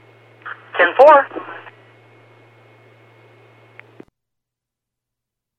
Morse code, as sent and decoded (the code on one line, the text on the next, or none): none